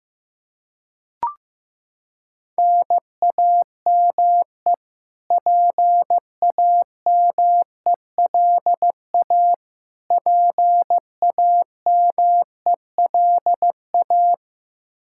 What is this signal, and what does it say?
Morse code, heard as a keyed tone, about 15 wpm, 700 Hz. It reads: NAME PAMELA PAMELA
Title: Practice QSO - Exchange 1 - Beginning - 15wpm